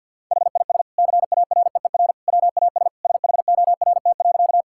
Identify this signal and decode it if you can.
Morse code, heard as a keyed tone, about 50 wpm, 700 Hz. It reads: HIS CAREER WAS SHORT.
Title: Sentences from Top 1000 Words - Review Part 6 - 50wpm